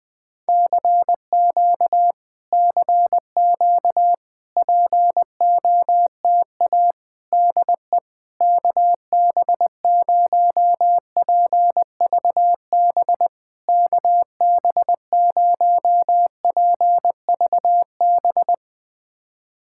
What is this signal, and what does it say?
Morse code, heard as a keyed tone, about 20 wpm, 700 Hz. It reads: CQ CQ POTA DE KB0PVB KB0PVB
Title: POTA QSO Practice - 20wpm